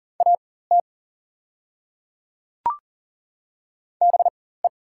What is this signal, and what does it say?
Morse code, heard as a keyed tone, about 40 wpm, spaced as for 19 wpm, 690 Hz. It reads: AT BE
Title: 4x Character Spacing - Top 200 Words - 40wpm